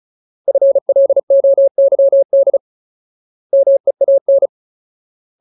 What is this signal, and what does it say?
Morse code, heard as a keyed tone, about 34 wpm, 550 Hz. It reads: FLOYD MEAN
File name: Ultimate Word List / 35WPM Ultimate